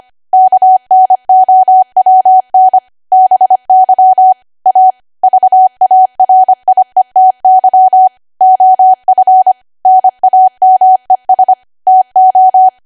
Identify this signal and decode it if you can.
Morse code, heard as a keyed tone, about 25 wpm, 750 Hz. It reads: KNOWN BY A VARIETY OF NAMES TO
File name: Archive / 25 WPM / 25WPM